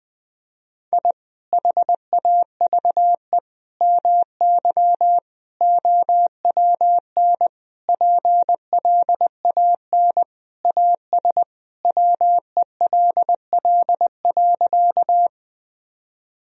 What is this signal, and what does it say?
Morse code, heard as a keyed tone, about 20 wpm, 700 Hz. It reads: I HAVE MY OWN PLAN AS WELL.
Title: Sentences from Top 1000 Words - Review Part 2 - 20wpm